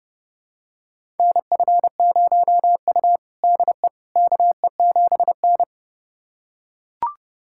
Morse code, heard as a keyed tone, about 30 wpm, 700 Hz. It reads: NF0U DE KE7N
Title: USA Callsign DE Callsign - 4 Characters - 30wpm